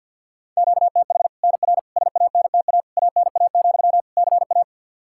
Morse code, heard as a keyed tone, about 52 wpm, 700 Hz. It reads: QTH NR SANTA ANA, CA
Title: Practice QSO - Exchange 1 - Beginning - 50wpm